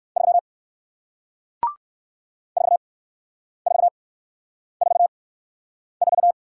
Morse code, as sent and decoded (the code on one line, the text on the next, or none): ...-- / ....- ....- ....- ....-
3 4444